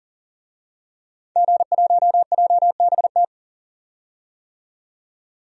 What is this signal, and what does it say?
Morse code, heard as a keyed tone, about 38 wpm, 700 Hz. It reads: G1JBT
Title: DXCC-100 Call Signs - 5 Characters - 40wpm